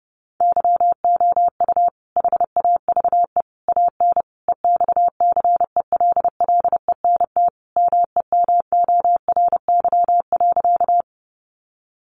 Morse code, heard as a keyed tone, about 30 wpm, 700 Hz. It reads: YOU HAVE AN EXCELLENT MEMORY.
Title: Sentences from Top 1300 Words 30wpm